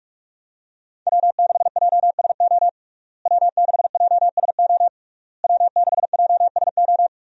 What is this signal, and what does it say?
Morse code, heard as a keyed tone, about 46 wpm, 700 Hz. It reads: W6JSO W6JSO W6JSO
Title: USA Call Signs - 5 Characters - Repeated 3x - 45wpm